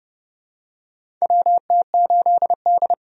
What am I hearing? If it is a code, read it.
Morse code, heard as a keyed tone, about 31 wpm, 700 Hz. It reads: WT8D
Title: USA Call Signs - 4 Characters - 30wpm